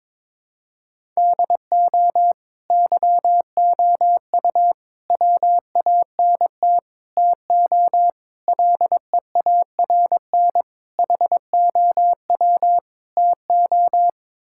Morse code, heard as a keyed tone, about 22 wpm, 700 Hz. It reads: DO YOU WANT TO LEARN HOW TO
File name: practice / 22wpm / Sets of 7 Words - 22wpm